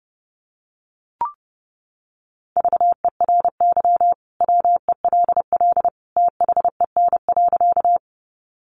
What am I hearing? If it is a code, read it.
Morse code, heard as a keyed tone, about 30 wpm, 700 Hz. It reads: VERY WELL THEN.
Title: Sentences from Top 200 Words 30wpm